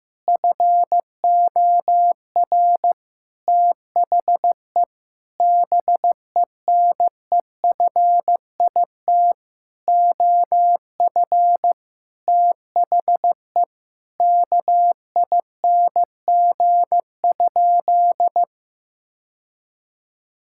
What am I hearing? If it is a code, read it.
Morse code, heard as a keyed tone, about 14 wpm, 700 Hz. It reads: FOR THE BENEFIT OF THE KING?
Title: Sentences from Top 1000 Words - Review Part 6 - 15wpm